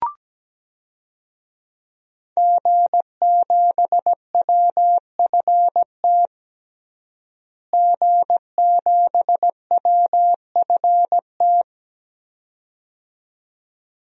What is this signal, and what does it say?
Morse code, heard as a keyed tone, about 17 wpm, 700 Hz. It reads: G7WFT G7WFT
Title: DXCC-100 Call Signs - 5 Characters - Repeated 2x - 17wpm